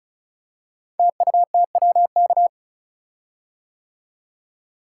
Morse code, heard as a keyed tone, about 36 wpm, 700 Hz. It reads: TUTWK
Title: Code Groups - 5 Letters - 35wpm